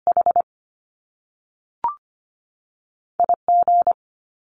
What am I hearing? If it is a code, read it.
Morse code, heard as a keyed tone, about 25 wpm, 700 Hz. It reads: H I G
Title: Top Two Letter Combinations in Words 25wpm